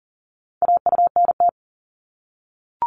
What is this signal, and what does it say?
Morse code, heard as a keyed tone, about 40 wpm, 700 Hz. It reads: AUNT
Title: Multi-letter Phonemes Repeated 3x 40wpm